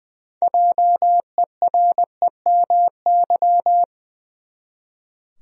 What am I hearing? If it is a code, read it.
Morse code, heard as a keyed tone, about 20 wpm, 700 Hz. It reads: JEREMY
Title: USA Common Names - 20wpm